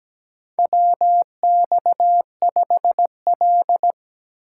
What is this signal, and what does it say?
Morse code, heard as a keyed tone, about 17 wpm, 700 Hz. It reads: WX5L
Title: USA Call Signs - 4 Characters - 8x Character Spacing - 17wpm